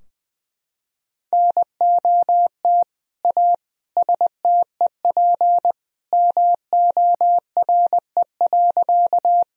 Morse code, heard as a keyed tone, about 21 wpm, 700 Hz. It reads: NOT A STEP MORE.
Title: Sentences from Top 1000 Words - Review Part 2 - 20wpm